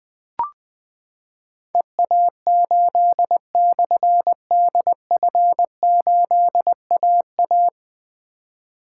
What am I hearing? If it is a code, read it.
Morse code, heard as a keyed tone, about 20 wpm, 700 Hz. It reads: EA8/DF8AA
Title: Difficult Call Signs 20wpm